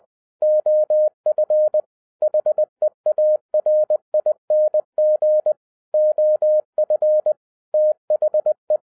Morse code, heard as a keyed tone, about 20 wpm, 610 Hz. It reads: OF HEARING OF THE
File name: 2011-07-mp3 / n58 BACK FROM A BIG ACCIDENT 20wpm